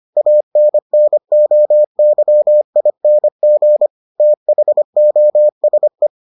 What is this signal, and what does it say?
Morse code, heard as a keyed tone, about 25 wpm, 600 Hz. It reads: ANNOYING THOSE